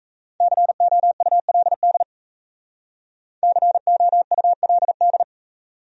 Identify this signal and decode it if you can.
Morse code, heard as a keyed tone, about 36 wpm, 700 Hz. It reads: COULD COULD